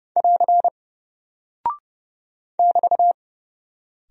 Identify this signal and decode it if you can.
Morse code, heard as a keyed tone, about 30 wpm, 700 Hz. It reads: <AR> <BT>
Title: Single Character (Letter:Number:Punctuation:Prosign) 30wpm